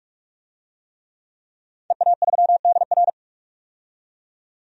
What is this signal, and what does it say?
Morse code, heard as a keyed tone, about 46 wpm, 700 Hz. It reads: EA3DR
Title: DXCC-100 Call Signs - 5 Characters - 45wpm